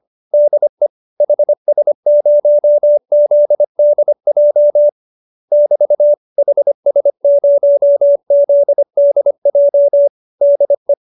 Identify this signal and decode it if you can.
Morse code, heard as a keyed tone, about 25 wpm, 570 Hz. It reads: DE HS0ZDJ <BT> HS0ZDJ DE